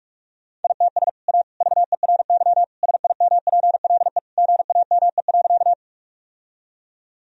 Morse code, heard as a keyed tone, about 46 wpm, 700 Hz. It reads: ITS A VERY SIMPLE GAME.
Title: Sentences from Top 1000 Words - Review Part 3 - 45wpm